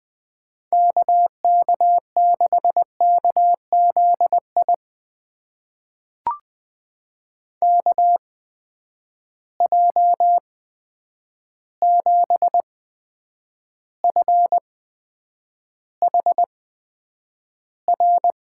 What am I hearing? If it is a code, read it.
Morse code, heard as a keyed tone, about 20 wpm, spaced as for 9 wpm, 700 Hz. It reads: KK6KZI KJ7FHR
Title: USA Call Signs - 6 Characters - 8x Character Spacing - 20wpm